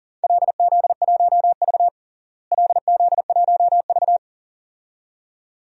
Morse code, heard as a keyed tone, about 40 wpm, 700 Hz. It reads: LZ1V LZ1V
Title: DXCC-100 Call Signs - 4 Characters - Repeated 2x - 40wpm